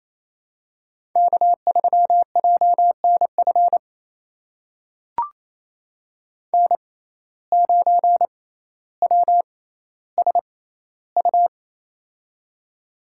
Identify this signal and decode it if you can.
Morse code, heard as a keyed tone, about 28 wpm, spaced as for 14 wpm, 700 Hz. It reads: K3JNF N9WSU
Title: USA Call Signs - 5 Characters - 6x Character Spacing - 28wpm